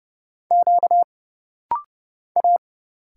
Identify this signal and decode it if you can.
Morse code, heard as a keyed tone, about 30 wpm, 700 Hz. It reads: Q A
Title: Single Letters - Rapid-Fire 30wpm